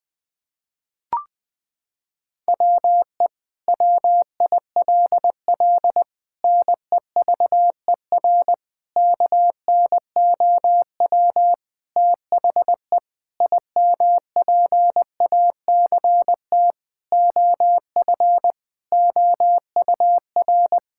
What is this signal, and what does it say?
Morse code, heard as a keyed tone, about 21 wpm, 700 Hz. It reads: WE WILL NEVER KNOW THE IMPACT OF OUR
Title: Sentences from Top 1000 Words - Review Part 6 - 20wpm